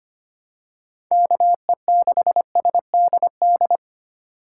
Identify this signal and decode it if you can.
Morse code, heard as a keyed tone, about 25 wpm, 700 Hz. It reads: KE6SDD
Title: USA Call Signs - 6 Characters - 8x Character Spacing - 25wpm